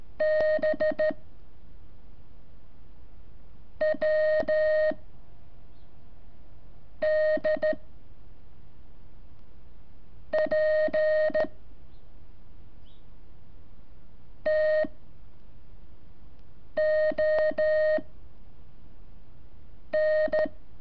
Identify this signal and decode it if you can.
Morse code, heard as a keyed tone, about 13 wpm, 620 Hz. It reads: B W D P T O N
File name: curso / ejercicio 27